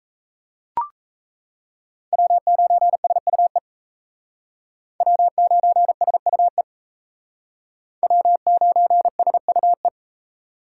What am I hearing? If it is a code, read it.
Morse code, heard as a keyed tone, about 35 wpm, 700 Hz. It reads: W9SUE W9SUE W9SUE
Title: Speed Racing - USA 1x3 Call Signs 28wpm